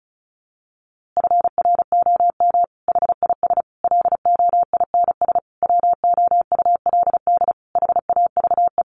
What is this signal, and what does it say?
Morse code, heard as a keyed tone, about 34 wpm, 700 Hz. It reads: FROM HIS LOINS WOULD HAVE
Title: A Sound Of Thunder 35wpm